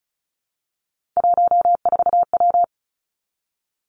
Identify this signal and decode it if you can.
Morse code, heard as a keyed tone, about 36 wpm, 700 Hz. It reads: 14W
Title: Three Challenging Letter-Number Combinations 35wpm